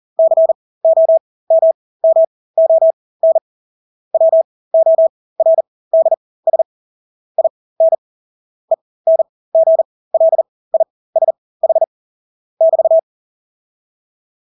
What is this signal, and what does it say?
Morse code, heard as a keyed tone, about 40 wpm, spaced as for 20 wpm, 650 Hz. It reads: COMMON WORDS IN ENGLISH <BT>
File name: Engelske ord 40-20 wpm